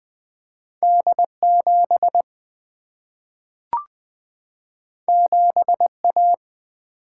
Morse code, heard as a keyed tone, about 20 wpm, 700 Hz. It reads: D 7 7 A
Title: Single-Letter Single-Number 20wpm